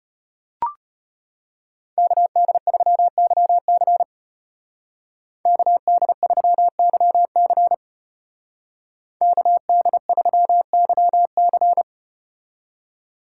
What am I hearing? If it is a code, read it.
Morse code, heard as a keyed tone, about 34 wpm, 700 Hz. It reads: KD3YC KD3YC KD3YC